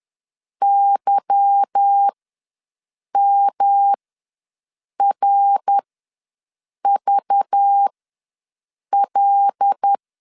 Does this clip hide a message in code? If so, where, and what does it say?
Yes: Morse code from 0.6 to 9.9 s, heard as a keyed tone, about 10 wpm, spaced as for 6 wpm, 800 Hz. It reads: YMRVL